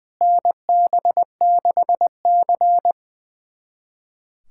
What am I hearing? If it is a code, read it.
Morse code, heard as a keyed tone, about 20 wpm, 700 Hz. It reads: NB6C
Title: USA Callsign DE Callsign - 4 Characters - 20wpm